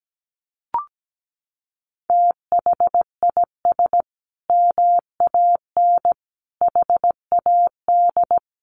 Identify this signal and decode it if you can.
Morse code, heard as a keyed tone, about 17 wpm, 700 Hz. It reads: THIS MAN HAD
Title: Sentences from Top 1200 Words 17wpm